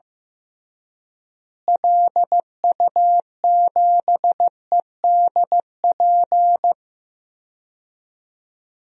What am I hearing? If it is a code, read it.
Morse code, heard as a keyed tone, about 15 wpm, 700 Hz. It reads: LU7EDP
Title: DXCC-100 Call Signs - 6 Characters - Repeated 2x - 15wpm